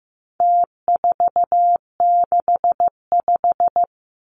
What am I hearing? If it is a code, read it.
Morse code, heard as a keyed tone, about 15 wpm, 700 Hz. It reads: T465
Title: Numbers - Contest Style - 4-Digit 15wpm